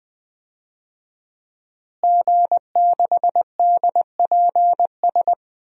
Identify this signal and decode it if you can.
Morse code, heard as a keyed tone, about 20 wpm, 700 Hz. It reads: G6DPS
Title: DXCC-100 Call Signs - 5 Characters - Repeated 2x - 20wpm